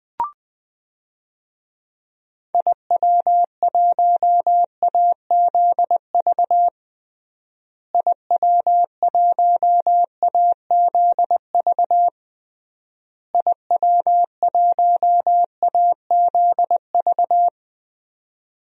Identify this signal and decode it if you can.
Morse code, heard as a keyed tone, about 20 wpm, 700 Hz. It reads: IW1AZV IW1AZV IW1AZV
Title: DXCC-100 Call Signs - 6 Characters - Repeated 3x - 20wpm